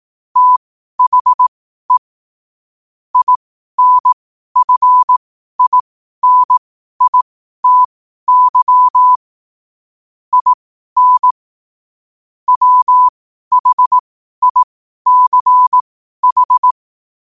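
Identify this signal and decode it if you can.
Morse code, heard as a keyed tone, about 18 wpm, spaced as for 12 wpm, 1000 Hz. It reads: THE INFINITY IN WHICH
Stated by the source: Morse code at 12 wpm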